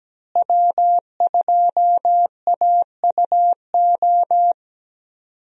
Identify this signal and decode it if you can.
Morse code, heard as a keyed tone, about 17 wpm, 700 Hz. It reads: W2AUO
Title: USA Call Signs - 5 Characters - Repeated 3x - 17wpm